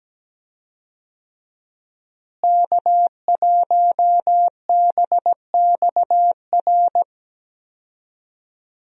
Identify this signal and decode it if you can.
Morse code, heard as a keyed tone, about 17 wpm, 700 Hz. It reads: K1BXR
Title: USA Call Signs - 5 Characters - Repeated 3x - 17wpm